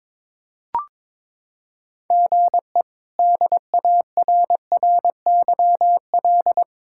Morse code, heard as a keyed tone, about 22 wpm, 700 Hz. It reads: GE DARRYL
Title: K1USN SST QSO Practice - 22wpm